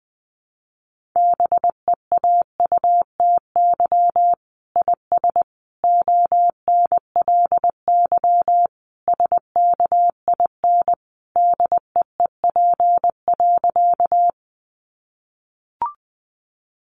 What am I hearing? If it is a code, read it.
Morse code, heard as a keyed tone, about 20 wpm, 700 Hz. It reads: BEAUTY IS ONLY SKIN DEEP.
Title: Sentences from Top 1300 Words 20wpm